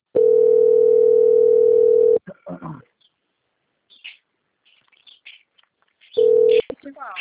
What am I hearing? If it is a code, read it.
Ringback tone (North America), 440 + 480 Hz, with speech.
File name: out- - - - - .93761